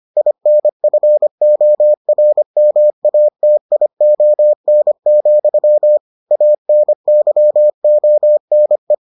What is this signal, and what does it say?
Morse code, heard as a keyed tone, about 25 wpm, 600 Hz. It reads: INFORMATION, ANYONE